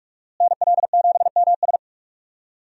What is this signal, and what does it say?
Morse code, heard as a keyed tone, about 46 wpm, 700 Hz. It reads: NL7MS
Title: USA Call Signs - 5 Characters - Repeated 3x - 45wpm